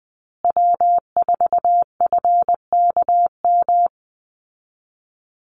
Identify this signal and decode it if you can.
Morse code, heard as a keyed tone, about 20 wpm, 700 Hz. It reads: W4FKM
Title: Call Signs Repeated 3x - 20wpm